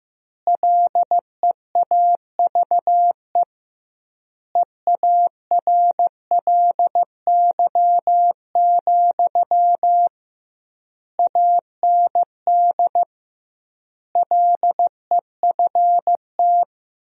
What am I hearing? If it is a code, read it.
Morse code, heard as a keyed tone, about 16 wpm, 700 Hz. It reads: LEAVE EARLY, AND LEFT
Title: 2x Word-Spacing - Sentences from Top 400 Words - 15wpm